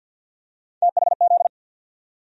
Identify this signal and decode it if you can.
Morse code, heard as a keyed tone, about 47 wpm, 700 Hz. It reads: THZ
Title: USA Call Sign Postfixes - 3 Letters 50wpm